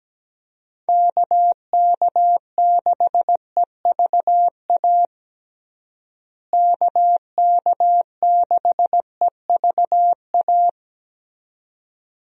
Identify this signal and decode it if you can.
Morse code, heard as a keyed tone, about 17 wpm, 700 Hz. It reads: KK6EVA KK6EVA
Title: USA Call Signs - 6 Characters - Repeated 2x - 17wpm